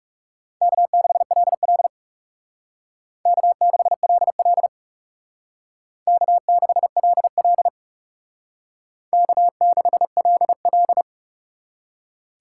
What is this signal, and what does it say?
Morse code, heard as a keyed tone, about 37 wpm, 700 Hz. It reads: K6LL K6LL K6LL K6LL
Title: Speed Racing - USA Call Signs 30wpm